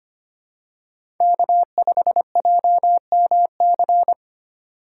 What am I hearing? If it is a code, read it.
Morse code, heard as a keyed tone, about 25 wpm, 690 Hz. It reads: K5JMC
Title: USA Call Signs - 4x Character Spacing - 25wpm